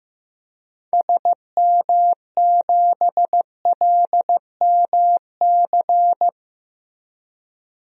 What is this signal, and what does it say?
Morse code, heard as a keyed tone, about 16 wpm, 700 Hz. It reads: SM7LMC
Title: DXCC-100 Call Signs - 6 Characters - Repeated 2x - 15wpm